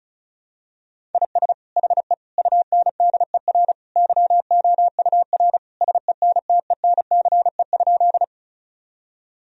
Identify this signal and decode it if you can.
Morse code, heard as a keyed tone, about 34 wpm, 700 Hz. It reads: IS HE UNDER YOUR SENTENCE?
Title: Sentences from Top 1000 Words - Review Part 1 - 35wpm